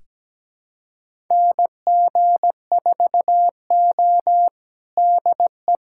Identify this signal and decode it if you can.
Morse code, heard as a keyed tone, about 17 wpm, 700 Hz. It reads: NG4O DE
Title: USA Callsign DE Callsign - 4 Characters - 17wpm